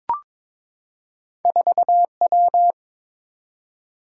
Morse code, heard as a keyed tone, about 22 wpm, 700 Hz. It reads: 4W
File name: practice / 22wpm / Two Letter-Number Combinations - 22wpm